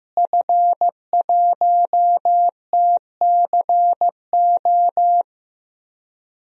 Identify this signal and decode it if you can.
Morse code, heard as a keyed tone, about 15 wpm, 700 Hz. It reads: F1TCO
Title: DXCC-100 Call Signs - 5 Characters - Repeated 3x - 15wpm